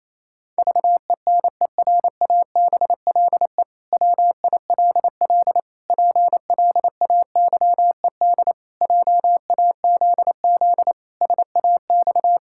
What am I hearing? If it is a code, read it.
Morse code, heard as a keyed tone, about 28 wpm, 700 Hz. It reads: VENERABLE WILL PLAYED JAZZ SAX
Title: Pangram Challenge v2 - 28wpm